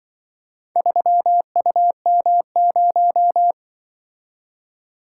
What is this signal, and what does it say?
Morse code, heard as a keyed tone, about 25 wpm, 700 Hz. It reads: 3UM0
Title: Speed-Racing - Four Letter-Number Combinations - 20wpm